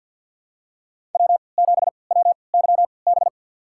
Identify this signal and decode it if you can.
Morse code, heard as a keyed tone, about 47 wpm, spaced as for 27 wpm, 700 Hz. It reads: W7WYB
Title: USA Call Signs - 5 Characters - 3x Character Spacing - 50wpm